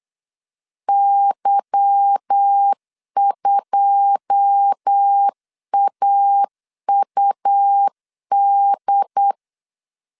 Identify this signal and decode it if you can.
Morse code, heard as a keyed tone, about 8 wpm, 800 Hz. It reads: Y2AUD
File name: CW / DET / 1 9 1500C S C 8wpm 5653